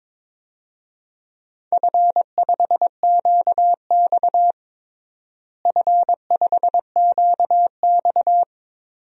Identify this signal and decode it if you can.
Morse code, heard as a keyed tone, about 22 wpm, 700 Hz. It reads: F5QX F5QX
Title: DXCC-100 Call Signs - 4 Characters - Repeated 2x - 22wpm